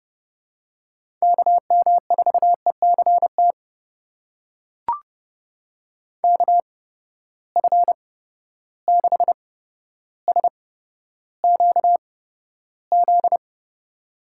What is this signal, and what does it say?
Morse code, heard as a keyed tone, about 30 wpm, spaced as for 13 wpm, 700 Hz. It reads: KM4ECT KF6SQZ
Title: USA Call Signs - 6 Characters - 8x Character Spacing - 30wpm